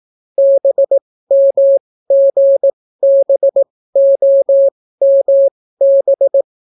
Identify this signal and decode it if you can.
Morse code, heard as a keyed tone, about 18 wpm, spaced as for 14 wpm, 550 Hz. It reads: BMGBOMB